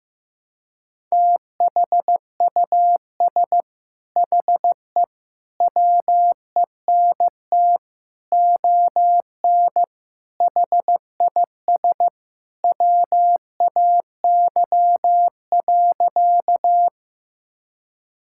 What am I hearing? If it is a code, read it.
Morse code, heard as a keyed tone, about 15 wpm, 700 Hz. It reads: THUS HE WENT ON HIS WAY.
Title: Sentences from Top 1000 Words - Review Part 5 - 15wpm